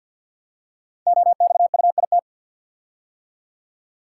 Morse code, heard as a keyed tone, about 56 wpm, 690 Hz. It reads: OXUIT
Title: Code Groups - 5 Letters - 50wpm